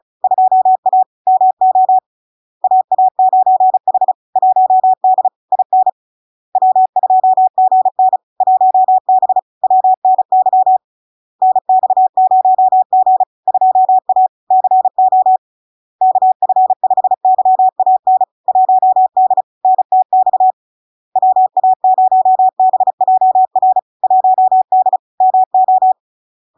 Morse code, heard as a keyed tone, about 35 wpm, 750 Hz. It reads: 2A MO AA9H 1D IN W2GN 1B WNY NX0G 2A CO KF5YAN 1D NTX WA0BJR 1D MO
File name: LARK-FD-2022-Log-in-CW-35wpm-750-Hz